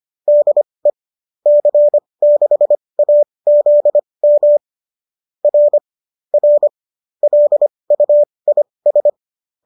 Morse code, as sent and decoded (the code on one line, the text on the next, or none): -.. / -.-. -.... .- --.. -- / .-. / .-. / .-.. ..- .. ...
DE C6AZM R R LUIS